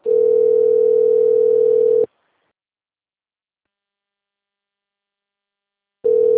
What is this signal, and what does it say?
Ringback tone (North America), 440 + 480 Hz.
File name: rg-1600-unknown- - - .15020